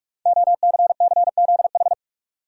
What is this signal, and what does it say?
Morse code, heard as a keyed tone, about 44 wpm, 690 Hz. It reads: OCCZH
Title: Code Groups - 5 Letters - 45wpm